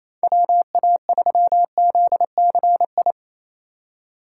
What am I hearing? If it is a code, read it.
Morse code, heard as a keyed tone, about 28 wpm, 700 Hz. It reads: WA3ZCI
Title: USA Call Signs - 6 Characters - 3x Character Spacing - 28wpm